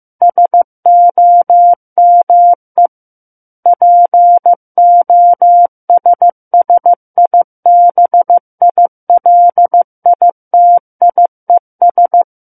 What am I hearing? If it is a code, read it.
Morse code, heard as a keyed tone, about 15 wpm, 700 Hz. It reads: SOME POSSIBILITIES
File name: Book One / 06-Chapter 6 - 15 WPM